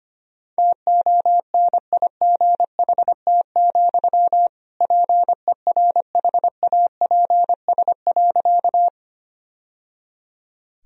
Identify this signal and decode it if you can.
Morse code, heard as a keyed tone, about 25 wpm, 700 Hz. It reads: TONIGHT, PERHAPS.
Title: Sentences from Top 1000 Words - Review Part 3 - 25wpm